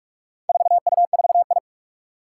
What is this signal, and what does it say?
Morse code, heard as a keyed tone, about 41 wpm, 700 Hz. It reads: 4U4I
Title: Four Challenging Letter-Number Combinations 45wpm